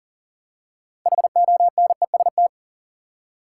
Morse code, heard as a keyed tone, about 40 wpm, 700 Hz. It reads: HONEST
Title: Six Letter Words - 40wpm